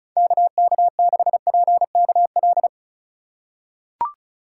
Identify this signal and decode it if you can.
Morse code, heard as a keyed tone, about 34 wpm, 700 Hz. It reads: KK6PKL
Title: USA Call Signs - 3x Character Spacing - 35wpm